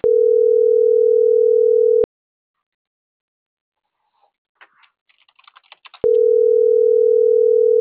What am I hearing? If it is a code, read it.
Ringback tone (North America), 440 + 480 Hz.